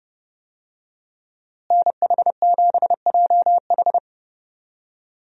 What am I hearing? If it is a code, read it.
Morse code, heard as a keyed tone, about 30 wpm, 700 Hz. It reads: NH7JH
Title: USA Call Signs - 5 Characters - Repeated 2x - 30wpm